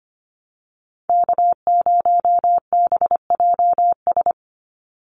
Morse code, heard as a keyed tone, about 25 wpm, 700 Hz. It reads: K0BJS
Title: USA Call Signs 25wpm